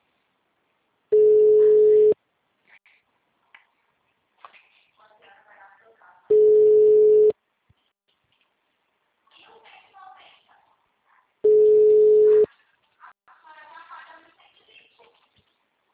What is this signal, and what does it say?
Ringback tone (Europe, 425 Hz), with speech.